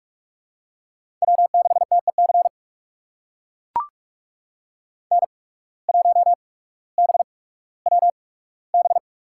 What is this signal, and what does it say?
Morse code, heard as a keyed tone, about 44 wpm, spaced as for 19 wpm, 700 Hz. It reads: W6TEC N1BWB
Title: USA Call Signs - 5 Characters - 8x Character Spacing - 45wpm